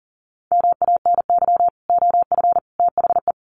Morse code, heard as a keyed tone, about 40 wpm, 700 Hz. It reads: MANY OF THE
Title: Sets of 3 Words 40wpm